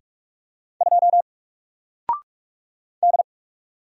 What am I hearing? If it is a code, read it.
Morse code, heard as a keyed tone, about 47 wpm, 700 Hz. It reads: 2 D
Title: Single Character (Letter:Number:Punctuation:Prosign) Alt 45wpm